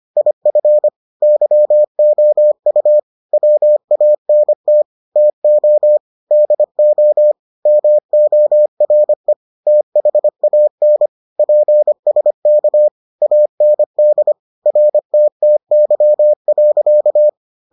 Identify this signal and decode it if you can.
Morse code, heard as a keyed tone, about 25 wpm, 600 Hz. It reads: IF YOU WANT TO DO MORE THAN PSK AND RTTY.